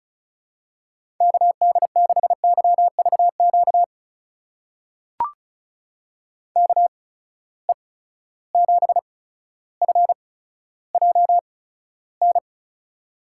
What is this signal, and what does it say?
Morse code, heard as a keyed tone, about 36 wpm, spaced as for 15 wpm, 700 Hz. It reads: KD6YVQ KE7FJN
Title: USA Call Signs - 6 Characters - 8x Character Spacing - 35wpm